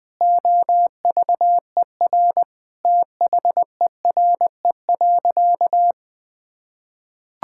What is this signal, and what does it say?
Morse code, heard as a keyed tone, about 21 wpm, 700 Hz. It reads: OVER THERE.
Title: Sentences with 3-Word Phrases - 20wpm